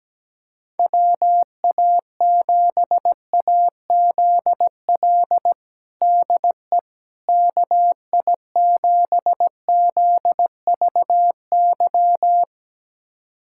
Morse code, heard as a keyed tone, about 17 wpm, 700 Hz. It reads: WA7AZL DE KI7ZVY
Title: USA Callsign DE Callsign - 6 Characters - 17wpm